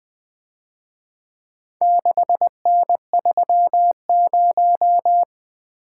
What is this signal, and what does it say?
Morse code, heard as a keyed tone, about 24 wpm, 700 Hz. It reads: 6N30
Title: Speed-Racing - Four Letter-Number Combinations - 20wpm